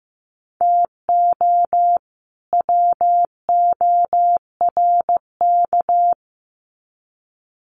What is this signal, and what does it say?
Morse code, heard as a keyed tone, about 15 wpm, 700 Hz. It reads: TO WORK
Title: Sets of 2 Words 15wpm